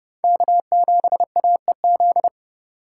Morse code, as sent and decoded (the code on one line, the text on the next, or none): -.- --... .- . --..
K7AEZ